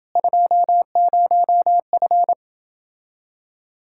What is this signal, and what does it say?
Morse code, heard as a keyed tone, about 29 wpm, 700 Hz. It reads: 20F